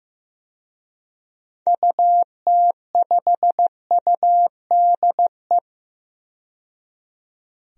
Morse code, heard as a keyed tone, about 15 wpm, 700 Hz. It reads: UT5UDE
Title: DXCC-100 Call Signs - 6 Characters - 15wpm